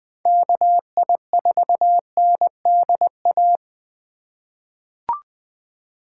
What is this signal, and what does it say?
Morse code, heard as a keyed tone, about 20 wpm, 700 Hz. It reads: KI4NDA
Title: USA Call Signs - 6 Characters - 3x Character Spacing - 20wpm